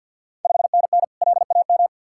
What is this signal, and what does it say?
Morse code, heard as a keyed tone, about 52 wpm, 690 Hz. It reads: 5NN LAM
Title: Ontario QSO Party Exchange 50wpm